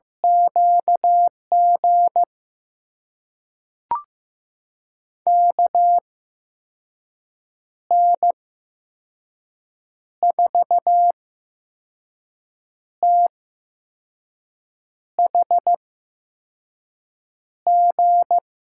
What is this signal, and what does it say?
Morse code, heard as a keyed tone, about 15 wpm, spaced as for 5 wpm, 700 Hz. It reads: QG KN4THG
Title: USA Call Signs - 6 Characters - 8x Character Spacing - 15wpm